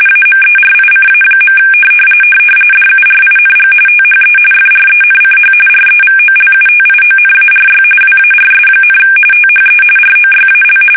RTTY, about 45.45 baud. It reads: AMERICA. DURING THE FRENCH REVOLUTIONARY WARS, THE ROYAL
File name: fsk850 / usnavy1